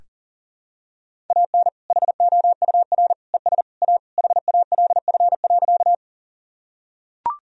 Morse code, heard as a keyed tone, about 40 wpm, 700 Hz. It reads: AN HOUR ES A HALF.
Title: Sentences from Top 400 Words and Abbreviations 40wpm